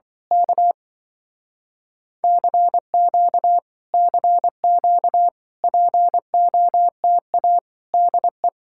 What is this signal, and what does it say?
Morse code, heard as a keyed tone, about 24 wpm, 700 Hz. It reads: K CQ CQ POTA DE
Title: Speed-Racing Common QSO Phrases - 20wpm